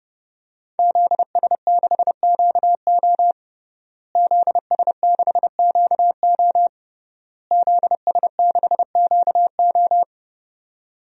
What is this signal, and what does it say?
Morse code, heard as a keyed tone, about 30 wpm, 700 Hz. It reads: ZS6QO ZS6QO ZS6QO
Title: DXCC-100 Call Signs - 5 Characters - Repeated 3x - 30wpm